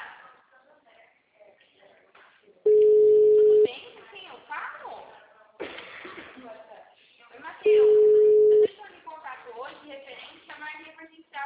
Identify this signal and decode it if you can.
Ringback tone (Europe, 425 Hz), with speech.